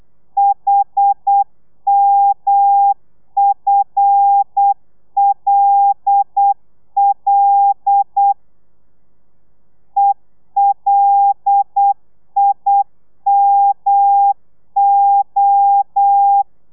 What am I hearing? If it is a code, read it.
Morse code, heard as a keyed tone, about 8 wpm, 800 Hz. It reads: HMFLL ELIMO
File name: telegrafia / i2tfj / Lez5 35